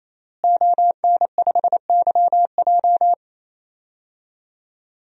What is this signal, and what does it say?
Morse code, heard as a keyed tone, about 28 wpm, 700 Hz. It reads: ON5YJ
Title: DXCC-100 Call Signs - 5 Characters - Repeated 3x - 28wpm